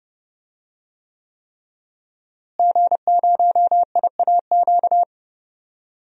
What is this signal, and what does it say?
Morse code, heard as a keyed tone, about 30 wpm, 700 Hz. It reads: G0IAQ
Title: DXCC-100 Call Signs - 5 Characters - Repeated 3x - 30wpm